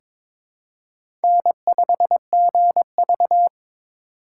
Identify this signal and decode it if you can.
Morse code, heard as a keyed tone, about 22 wpm, 700 Hz. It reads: N5GV
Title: USA Call Signs - 4 Characters - 4x Character Spacing - 22wpm